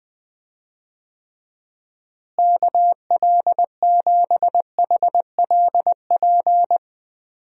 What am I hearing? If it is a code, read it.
Morse code, heard as a keyed tone, about 20 wpm, 700 Hz. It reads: KL7HLP
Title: DXCC-100 Call Signs - 6 Characters - Repeated 3x - 20wpm